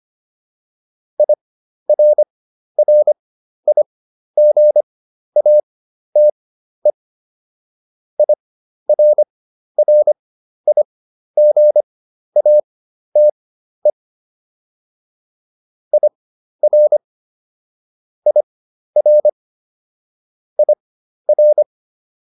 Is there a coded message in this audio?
Yes: Morse code from 1.2 to 21.6 s, heard as a keyed tone, about 25 wpm, spaced as for 12 wpm, 600 Hz. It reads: IRRIGATE IRRIGATE IR IR IR